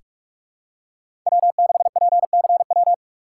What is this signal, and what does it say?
Morse code, heard as a keyed tone, about 45 wpm, 700 Hz. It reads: W6PCW
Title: USA Call Signs - 5 Characters - 3x Character Spacing - 45wpm